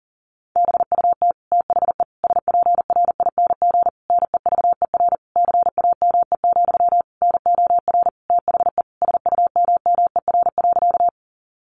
Morse code, heard as a keyed tone, about 37 wpm, 700 Hz. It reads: BUT THE SPRING NEVER CAME, NOR THE SUMMER.
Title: Sentences from Top 1500 Words 40wpm